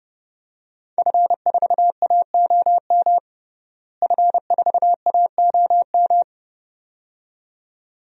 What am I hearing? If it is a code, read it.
Morse code, heard as a keyed tone, about 30 wpm, 700 Hz. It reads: F4AOM F4AOM
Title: DXCC-100 Call Signs - 5 Characters - Repeated 2x - 30wpm